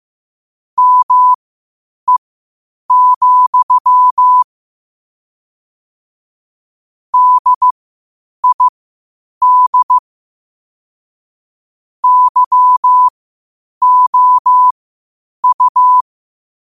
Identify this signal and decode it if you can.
Morse code, heard as a keyed tone, about 15 wpm, spaced as for 8 wpm, 1000 Hz. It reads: ME, DID YOU
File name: qotdfiles / 8wpm